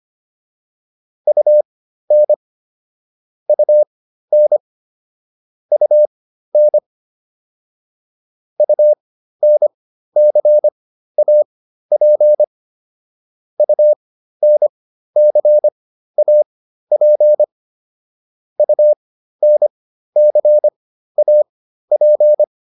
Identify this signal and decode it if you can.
Morse code, heard as a keyed tone, about 25 wpm, spaced as for 13 wpm, 600 Hz. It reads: UN UN UN UNCAP UNCAP UNCAP